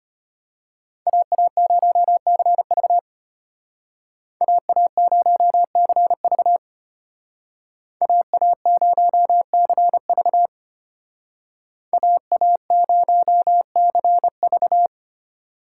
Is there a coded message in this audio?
Yes: Morse code from 1.1 to 14.9 s, heard as a keyed tone, about 33 wpm, 700 Hz. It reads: AA0CV AA0CV AA0CV AA0CV